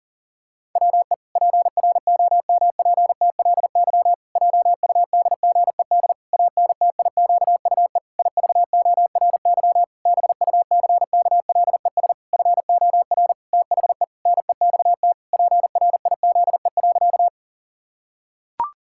Morse code, heard as a keyed tone, about 40 wpm, 700 Hz. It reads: WE PROMPTLY JUDGED ANTIQUE IVORY BUCKLES FOR THE NEXT PRIZE.
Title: Pangram Challenge v2 - 40wpm